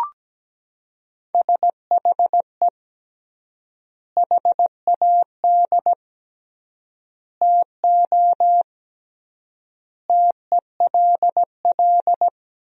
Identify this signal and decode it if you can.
Morse code, heard as a keyed tone, about 17 wpm, 700 Hz. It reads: SHE HAD TO TELL
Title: 3x Word-Spacing - Sentences from Top 200 Words - 17wpm